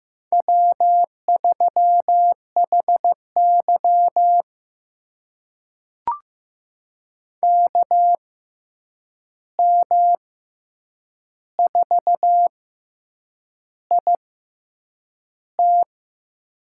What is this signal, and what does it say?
Morse code, heard as a keyed tone, about 15 wpm, spaced as for 7 wpm, 700 Hz. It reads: W3HY KM4IT
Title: USA Call Signs - 6x Character Spacing - 15wpm